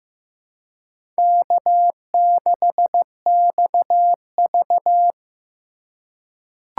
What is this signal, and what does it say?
Morse code, heard as a keyed tone, about 15 wpm, 700 Hz. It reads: K6XV
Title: USA Call Signs - 4 Characters - 3x Character Spacing - 15wpm